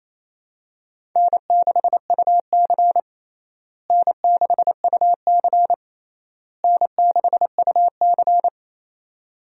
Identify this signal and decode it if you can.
Morse code, heard as a keyed tone, about 28 wpm, 700 Hz. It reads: N6UC N6UC N6UC
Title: USA Call Signs - 4 Characters - Repeated 3x - 28wpm